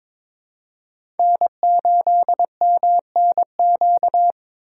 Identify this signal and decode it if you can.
Morse code, heard as a keyed tone, about 22 wpm, 700 Hz. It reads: N8MNQ
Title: USA Call Signs - 5 Characters - 3x Character Spacing - 22wpm